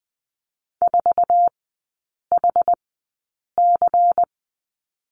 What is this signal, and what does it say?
Morse code, heard as a keyed tone, about 20 wpm, spaced as for 8 wpm, 700 Hz. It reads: 4HC
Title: 093 Lesson 16e - Letter c - Three Characters 20wpm